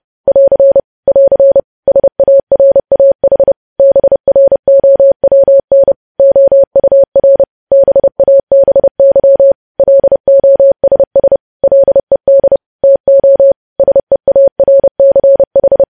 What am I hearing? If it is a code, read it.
Morse code, heard as a keyed tone, about 30 wpm, 550 Hz. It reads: <AR> <AR> SARAH BROWN OUR BABY LOSS LED TO SEARCH
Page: news headlines in morse code at 30 wpm